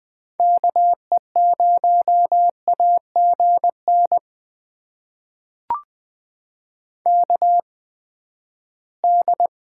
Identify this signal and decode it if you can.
Morse code, heard as a keyed tone, about 20 wpm, spaced as for 14 wpm, 700 Hz. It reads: KE0AGN KD
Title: USA Call Signs - 6 Characters - 8x Character Spacing - 20wpm